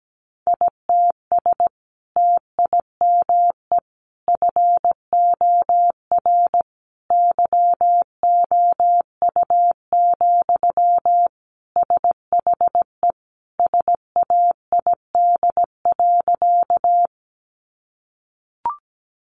Morse code, heard as a keyed tone, about 17 wpm, 700 Hz. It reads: ITS TIME FOR YOU, SHE SAID.
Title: Sentences from Top 600 Words 17wpm